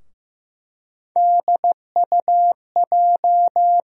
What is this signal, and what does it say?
Morse code, heard as a keyed tone, about 15 wpm, 700 Hz. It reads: DUJ
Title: USA Call Sign Postfixes - 3 Letters 15wpm